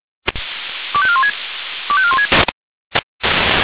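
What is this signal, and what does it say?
Five-tone selective call (ZVEI), heard as quick runs of five tones, tones of about 70 ms. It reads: 25416 25416
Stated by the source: Received by SDR as a ZVEI five-tone call.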